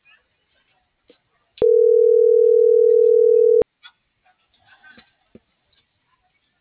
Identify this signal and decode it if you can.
Ringback tone (North America), 440 + 480 Hz.